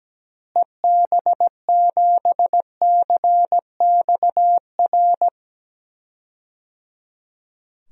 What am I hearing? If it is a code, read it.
Morse code, heard as a keyed tone, about 17 wpm, 700 Hz. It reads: EB7CXR
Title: DXCC-100 Call Signs - 6 Characters - Repeated 3x - 17wpm